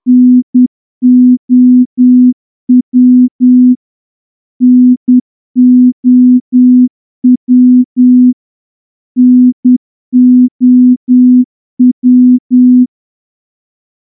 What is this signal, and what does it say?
Morse code, heard as a keyed tone, about 10 wpm, 250 Hz. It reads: NOW NOW NOW
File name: morse-training-10WPM